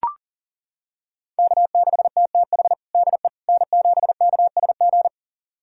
Morse code, heard as a keyed tone, about 37 wpm, 700 Hz. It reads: K6TTH DE N7KSG
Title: USA Callsign DE Callsign - 5 Characters - 40wpm